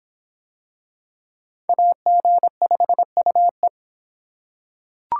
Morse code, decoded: AG5UE

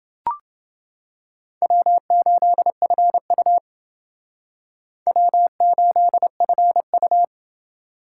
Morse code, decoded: W8FU W8FU

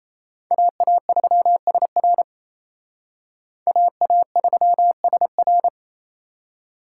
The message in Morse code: AA3SR AA3SR

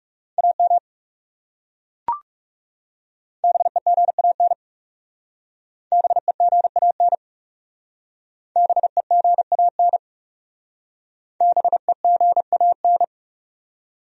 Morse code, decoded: AM BEGAN BEGAN BEGAN BEGAN

